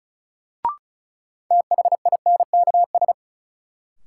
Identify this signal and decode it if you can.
Morse code, heard as a keyed tone, about 36 wpm, 700 Hz. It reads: THINKS
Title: Sentences from Top 1000 Words - Review Part 6 - New Word Drill - 35wpm